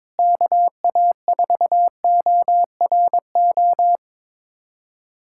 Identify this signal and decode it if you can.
Morse code, heard as a keyed tone, about 21 wpm, 700 Hz. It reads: KA4ORO
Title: USA Callsign DE Callsign 22wpm